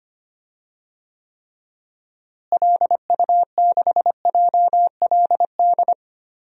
Morse code, heard as a keyed tone, about 25 wpm, 700 Hz. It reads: LU6JLD
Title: DXCC-100 Call Signs - 6 Characters - Repeated 3x - 25wpm